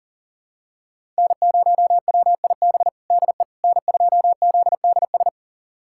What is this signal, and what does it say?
Morse code, heard as a keyed tone, about 40 wpm, 700 Hz. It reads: N0WIB DE N2ZDS
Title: USA Callsign DE Callsign - 5 Characters - 40wpm